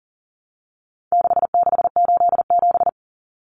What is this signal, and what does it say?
Morse code, heard as a keyed tone, about 40 wpm, 700 Hz. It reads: 6687
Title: Common Cut Numbers - 4-Digit 40wpm